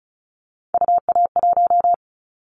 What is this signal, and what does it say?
Morse code, heard as a keyed tone, about 36 wpm, 700 Hz. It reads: UA1
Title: Callsign Prefixes for Top 100 DXCC - 35wpm